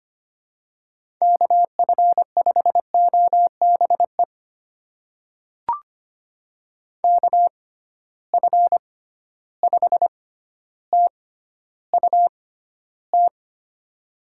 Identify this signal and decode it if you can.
Morse code, heard as a keyed tone, about 25 wpm, spaced as for 13 wpm, 700 Hz. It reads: KF5OBE KF5TUT
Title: USA Call Signs - 6 Characters - 6x Character Spacing - 25wpm